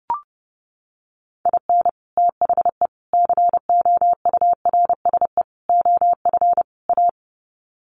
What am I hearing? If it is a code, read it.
Morse code, heard as a keyed tone, about 30 wpm, 700 Hz. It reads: IN THE COURSE OF A
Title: Sets of 5 Words Encore2 30wpm